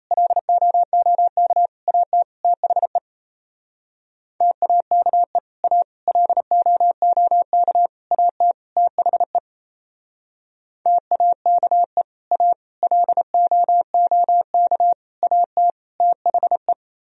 Morse code, heard as a keyed tone, about 36 wpm, 700 Hz. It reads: LOOK AT THE TAKE A LOOK AT THE TAKE A LOOK AT THE